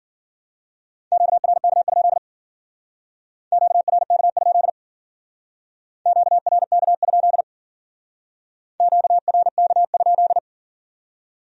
Morse code, heard as a keyed tone, about 52 wpm, 700 Hz. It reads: QRK? QRK? QRK? QRK?